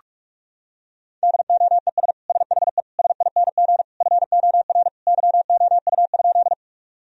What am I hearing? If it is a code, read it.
Morse code, heard as a keyed tone, about 46 wpm, 700 Hz. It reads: DOES SHE SING FOR YOU?
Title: Sentences from Top 1000 Words - Review Part 2 - 45wpm